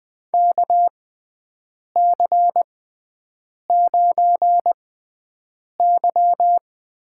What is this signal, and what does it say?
Morse code, heard as a keyed tone, about 20 wpm, spaced as for 7 wpm, 700 Hz. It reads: KC9Y
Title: USA Call Signs - 6 Characters - 6x Character Spacing - 20wpm